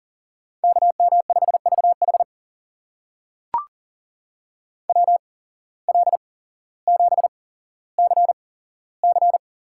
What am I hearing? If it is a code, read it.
Morse code, heard as a keyed tone, about 40 wpm, spaced as for 17 wpm, 700 Hz. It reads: KM5VH WL7CC
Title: USA Call Signs - 5 Characters - 8x Character Spacing - 40wpm